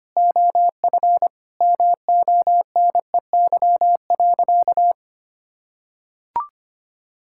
Morse code, heard as a keyed tone, about 25 wpm, 700 Hz. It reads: OF MONEY.
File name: practice / 25wpm / Sentences with 4-Word Phrases - 25wpm